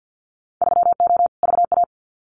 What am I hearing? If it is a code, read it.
Morse code, heard as a keyed tone, about 50 wpm, 700 Hz. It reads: 3O VA
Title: Winter Field Day Exchanges 50wpm